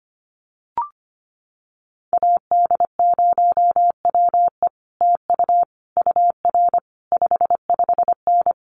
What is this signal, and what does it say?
Morse code, heard as a keyed tone, about 25 wpm, 700 Hz. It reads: AD0WE TU UR 55N
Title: Common QSO Phrases Encore 25wpm